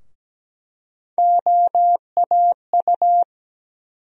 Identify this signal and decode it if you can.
Morse code, heard as a keyed tone, about 17 wpm, 700 Hz. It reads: OAU
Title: USA Call Sign Postfixes - 3 Letters 17wpm